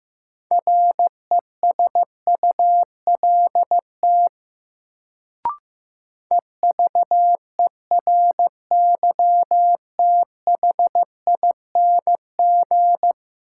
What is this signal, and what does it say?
Morse code, heard as a keyed tone, about 15 wpm, 700 Hz. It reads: RESULT EVERYTHING
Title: Sentences from Top 700 Words - New Word Drill - 15wpm